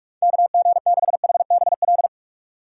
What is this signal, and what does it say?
Morse code, heard as a keyed tone, about 46 wpm, 700 Hz. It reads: KG6HBL
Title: USA Call Signs - 6 Characters - 2x Character Spacing - 45wpm